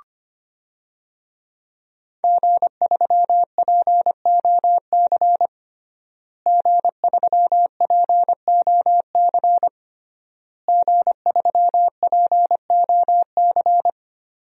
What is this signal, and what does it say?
Morse code, heard as a keyed tone, about 25 wpm, 700 Hz. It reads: G3POC G3POC G3POC
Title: DXCC-100 Call Signs - 5 Characters - Repeated 3x - 25wpm